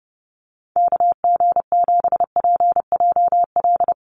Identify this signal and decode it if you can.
Morse code, heard as a keyed tone, about 30 wpm, 700 Hz. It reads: KG7PJL
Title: USA Call Signs Repeated 3x 30wpm